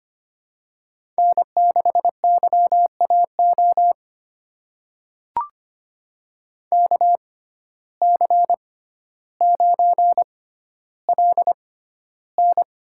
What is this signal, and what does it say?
Morse code, heard as a keyed tone, about 25 wpm, spaced as for 13 wpm, 690 Hz. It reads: N6YAO KC9LN
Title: USA Call Signs - 5 Characters - 6x Character Spacing - 25wpm